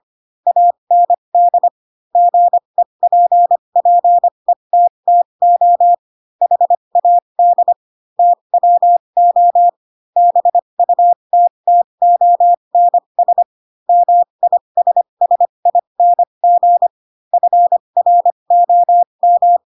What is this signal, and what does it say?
Morse code, heard as a keyed tone, about 25 wpm, spaced as for 22 wpm, 700 Hz. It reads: AND GEPPETTO HAD TWO BUTTONS MISSING FROM
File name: Pinocchio / Pinocchio ch2 18 25 ik1hge